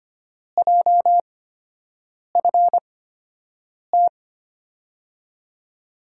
Morse code, heard as a keyed tone, about 25 wpm, spaced as for 7 wpm, 700 Hz. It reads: JFT
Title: USA Call Signs - 5 Characters - 8x Character Spacing - 25wpm